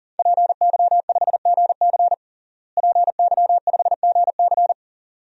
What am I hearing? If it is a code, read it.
Morse code, heard as a keyed tone, about 37 wpm, 700 Hz. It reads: PY5GC PY5GC
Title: DXCC-100 Call Signs - 5 Characters - Repeated 2x - 40wpm